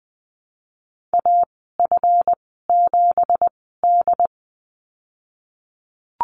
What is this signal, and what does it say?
Morse code, heard as a keyed tone, about 20 wpm, spaced as for 15 wpm, 700 Hz. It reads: AF7D
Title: 173 Lesson 28g - Number 9 - Call Signs 20wpm